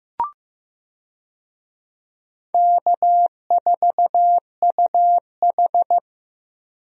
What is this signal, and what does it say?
Morse code, heard as a keyed tone, about 15 wpm, 700 Hz. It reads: K4UH
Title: USA Call Signs - 4 Characters - 15wpm